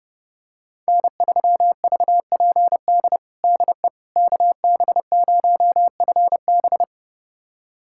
Morse code, heard as a keyed tone, about 32 wpm, 700 Hz. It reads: N3VPD DE KB0FB
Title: USA Callsign DE Callsign - 5 Characters - 30wpm